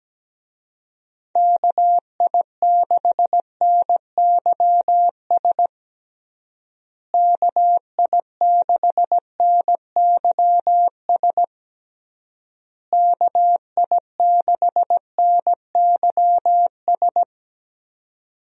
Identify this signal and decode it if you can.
Morse code, heard as a keyed tone, about 17 wpm, 700 Hz. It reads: KI6NYS KI6NYS KI6NYS
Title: USA Call Signs - 6 Characters - Repeated 3x - 17wpm